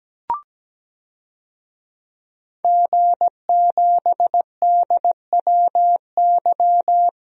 Morse code, heard as a keyed tone, about 17 wpm, 700 Hz. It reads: G7DWY
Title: DXCC-100 Call Signs - 5 Characters - 17wpm